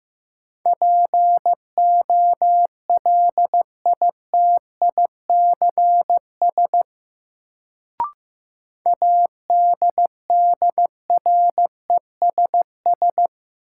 Morse code, heard as a keyed tone, about 15 wpm, 700 Hz. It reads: POLITICS ADDRESS
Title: Sentences from Top 1000 Words - New Word Drill - 15wpm